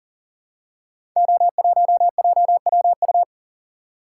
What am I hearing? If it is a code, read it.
Morse code, heard as a keyed tone, about 42 wpm, 690 Hz. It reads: O1JWU
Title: Code Groups - 5 Letter:Numbers - 40wpm